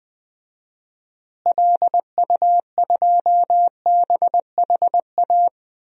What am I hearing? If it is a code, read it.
Morse code, heard as a keyed tone, about 20 wpm, 700 Hz. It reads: LU2BHA
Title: DXCC-100 Call Signs - 6 Characters - Repeated 2x - 20wpm